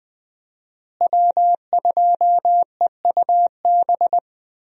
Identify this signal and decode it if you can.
Morse code, heard as a keyed tone, about 20 wpm, 700 Hz. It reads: W2EUB
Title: USA Call Signs - 5 Characters - Repeated 2x - 20wpm